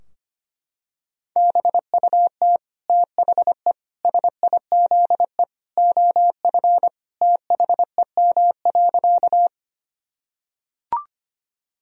Morse code, heard as a keyed tone, about 25 wpm, 700 Hz. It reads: BUT THE SIZE OF THEM.
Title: Sentences from Top 1000 Words - Review Part 2 - 25wpm